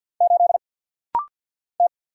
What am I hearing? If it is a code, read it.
Morse code, heard as a keyed tone, about 48 wpm, 700 Hz. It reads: 8 T
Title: Single Letter-Number - Rapid-Fire 50wpm